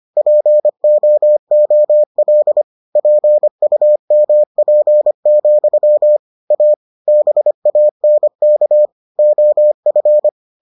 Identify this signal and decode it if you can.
Morse code, heard as a keyed tone, about 25 wpm, 600 Hz. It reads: POOL PUMP, A BANK OF